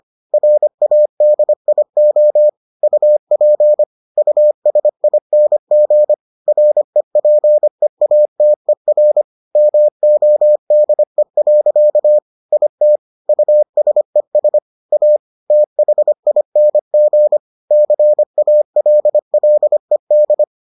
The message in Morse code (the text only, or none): RADIO UP USING REPEATER MODE. IT USES A THING CALLED